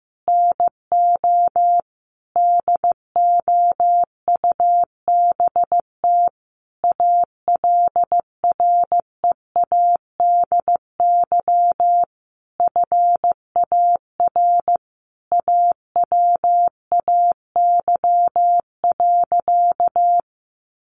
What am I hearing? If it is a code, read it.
Morse code, heard as a keyed tone, about 15 wpm, 700 Hz. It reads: NO DOUBT ALREADY FAR AWAY.
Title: Sentences from Top 1500 Words 15wpm